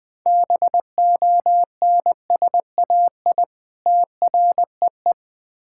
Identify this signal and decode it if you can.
Morse code, heard as a keyed tone, about 20 wpm, 700 Hz. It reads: BONSAI TREE
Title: Difficult Words with Hints - 20wpm